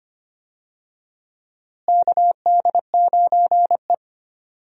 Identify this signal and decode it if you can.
Morse code, heard as a keyed tone, about 25 wpm, 700 Hz. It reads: KD9E
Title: USA Call Signs - 4 Characters - Repeated 3x - 25wpm